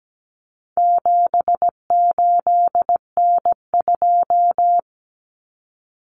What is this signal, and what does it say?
Morse code, heard as a keyed tone, about 17 wpm, 700 Hz. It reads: 78N2
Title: Common Cut Numbers - 4-Digit 17wpm